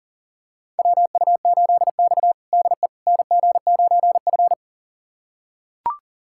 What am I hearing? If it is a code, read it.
Morse code, heard as a keyed tone, about 40 wpm, 700 Hz. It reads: WU8X DE NG9F
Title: USA Callsign DE Callsign - 4 Characters - 40wpm